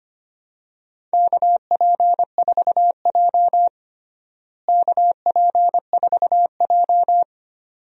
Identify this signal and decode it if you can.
Morse code, heard as a keyed tone, about 25 wpm, 700 Hz. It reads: KP4J KP4J
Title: DXCC-100 Call Signs - 4 Characters - Repeated 2x - 25wpm